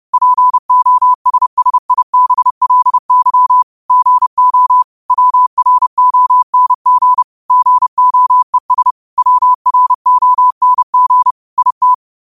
Morse code, heard as a keyed tone, about 30 wpm, 1000 Hz. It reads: POSSIBLY GO WRONG GOES WRONG IT
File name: qotdfiles / 30wpm